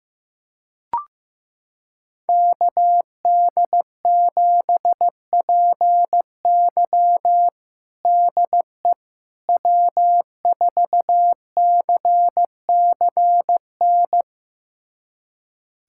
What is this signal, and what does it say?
Morse code, heard as a keyed tone, about 16 wpm, 700 Hz. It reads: KD7PY DE W4CCN
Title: USA Callsign DE Callsign - 5 Characters - 15wpm